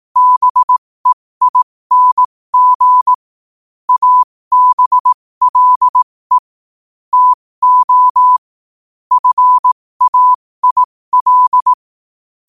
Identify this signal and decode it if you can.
Morse code, heard as a keyed tone, about 18 wpm, spaced as for 15 wpm, 1000 Hz. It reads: BEING ABLE TO FAIL